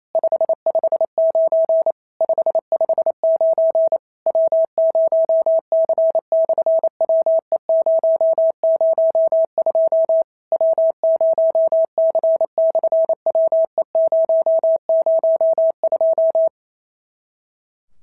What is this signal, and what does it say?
Morse code, heard as a keyed tone, about 28 wpm, 650 Hz. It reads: 559 559 W0C/WE002 W0C/WE002
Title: SOTA QSO Practice 28wpm